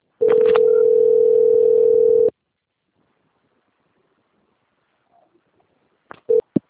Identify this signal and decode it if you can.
Ringback tone (North America), 440 + 480 Hz.